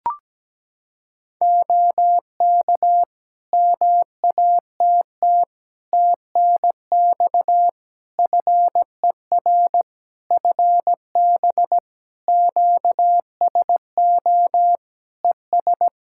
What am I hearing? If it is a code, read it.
Morse code, heard as a keyed tone, about 17 wpm, 700 Hz. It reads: OK MATT TNX FER FB QSO ES
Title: Practice QSO - Exchange 4 - Ending - 17wpm